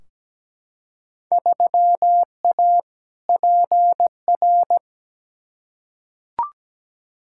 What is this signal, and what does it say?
Morse code, heard as a keyed tone, about 17 wpm, 700 Hz. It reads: 3A PR
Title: Field Day Exchanges Encore - 17wpm